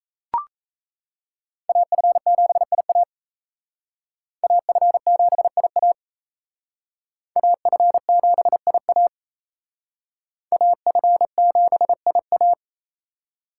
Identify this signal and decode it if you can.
Morse code, heard as a keyed tone, about 35 wpm, 700 Hz. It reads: AF7IA AF7IA AF7IA AF7IA
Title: Speed Racing - USA Call Signs 28wpm